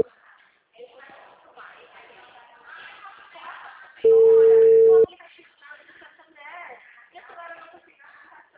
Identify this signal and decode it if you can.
Ringback tone (China, 450 Hz), with speech.